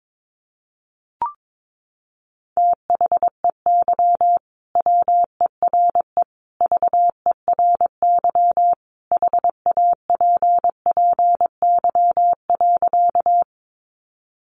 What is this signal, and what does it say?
Morse code, heard as a keyed tone, about 22 wpm, 700 Hz. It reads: THEY WERE VERY HAPPY.
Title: Sentences from Top 1000 Words 22wpm